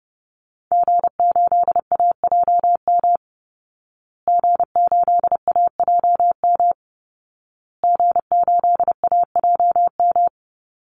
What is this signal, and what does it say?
Morse code, heard as a keyed tone, about 30 wpm, 700 Hz. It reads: G8AJM G8AJM G8AJM
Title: Call Signs Repeated 2x - 30wpm